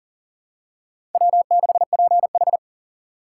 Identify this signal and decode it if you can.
Morse code, heard as a keyed tone, about 40 wpm, 700 Hz. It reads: W6PH
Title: USA Call Signs - 4 Characters - 2x Character Spacing - 40wpm